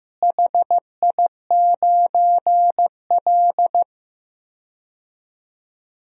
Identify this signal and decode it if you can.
Morse code, heard as a keyed tone, about 15 wpm, 700 Hz. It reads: HI9L
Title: DXCC-100 Call Signs - 4 Characters - Repeated 2x - 15wpm